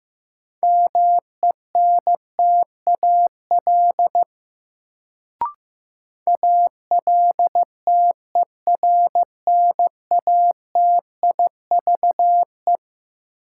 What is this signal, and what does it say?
Morse code, heard as a keyed tone, about 15 wpm, 700 Hz. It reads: MENTAL ALTERNATIVE
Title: Sentences from Top 1000 Words - Review Part 4 - New Word Drill - 15wpm